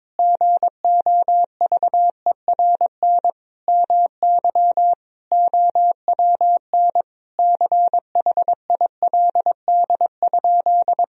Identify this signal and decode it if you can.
Morse code, heard as a keyed tone, about 21 wpm, 700 Hz. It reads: GOVERN MY OWN CHILD?
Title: Sentences from Top 1000 Words - Review Part 2 - 22wpm